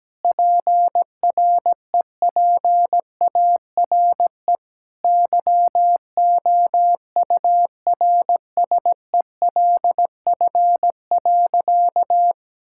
Morse code, heard as a keyed tone, about 17 wpm, 700 Hz. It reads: PREPARE YOURSELF.
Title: Sentences from Top 1000 Words - Review Part 6 - 17wpm